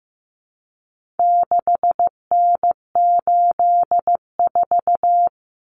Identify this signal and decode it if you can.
Morse code, heard as a keyed tone, about 15 wpm, 700 Hz. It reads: 6N84
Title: Common Cut Numbers - 4-Digit 15wpm